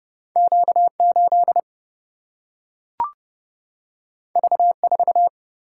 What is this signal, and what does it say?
Morse code, heard as a keyed tone, about 30 wpm, 700 Hz. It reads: Q 8 V 4
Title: Single-Letter Single-Number 30wpm